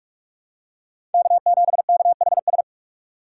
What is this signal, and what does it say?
Morse code, heard as a keyed tone, about 46 wpm, 700 Hz. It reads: K7KHS
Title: USA Call Signs - 5 Characters - 4x Character Spacing - 45wpm